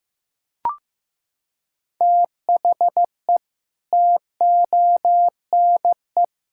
Morse code, heard as a keyed tone, about 15 wpm, 700 Hz. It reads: THE TONE
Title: Sentences from Top 1600 Words 15wpm